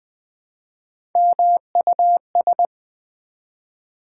Morse code, heard as a keyed tone, about 20 wpm, 700 Hz. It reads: MUS
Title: Build A Word - 1 Letter At A Time - 3 to 5-Letter Words - 20wpm